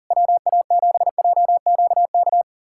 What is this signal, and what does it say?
Morse code, heard as a keyed tone, about 38 wpm, 700 Hz. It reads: WA7JQK
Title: USA Callsign DE Callsign 40wpm